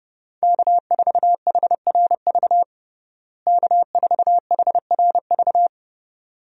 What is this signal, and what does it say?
Morse code, heard as a keyed tone, about 32 wpm, 700 Hz. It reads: K4HRV K4HRV
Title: USA Call Signs - 5 Characters - Repeated 2x - 30wpm